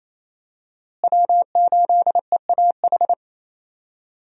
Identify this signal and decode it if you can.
Morse code, heard as a keyed tone, about 27 wpm, 700 Hz. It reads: W8EAH
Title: USA Call Signs - 5 Characters - Repeated 2x - 28wpm